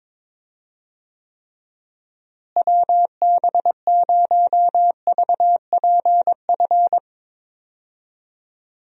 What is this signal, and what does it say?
Morse code, heard as a keyed tone, about 22 wpm, 700 Hz. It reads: WB0VPF
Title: USA Call Signs - 6 Characters - Repeated 3x - 22wpm